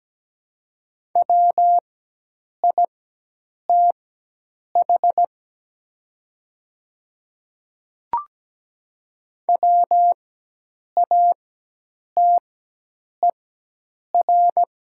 Morse code, heard as a keyed tone, about 17 wpm, spaced as for 8 wpm, 700 Hz. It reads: WITH WATER
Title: 4x Character Spacing - Top 200 Words - 17wpm